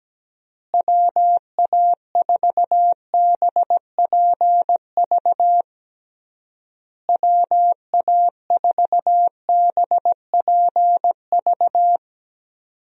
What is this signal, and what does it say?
Morse code, heard as a keyed tone, about 17 wpm, 700 Hz. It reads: WA4BPV WA4BPV
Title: USA Call Signs - 6 Characters - Repeated 3x - 17wpm